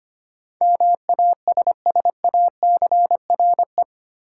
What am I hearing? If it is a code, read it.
Morse code, heard as a keyed tone, about 25 wpm, 700 Hz. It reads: MASSACRE
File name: 8 Letter Words - First 3 Letters Given - 25wpm